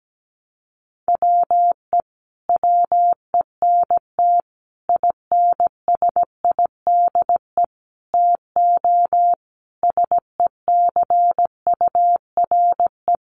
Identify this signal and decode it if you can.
Morse code, heard as a keyed tone, about 17 wpm, 700 Hz. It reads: WE WENT INSIDE TO SECURE
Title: Sentences from Top 1200 Words 17wpm